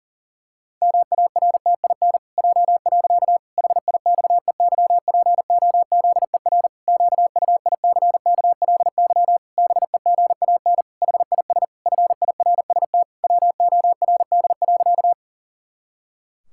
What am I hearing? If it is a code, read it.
Morse code, heard as a keyed tone, about 40 wpm, 700 Hz. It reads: MARTIN J. HIXEYPOZER QUICKLY BEGAN HIS FIRST WORD.
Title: Pangram Challenge v2 - 40wpm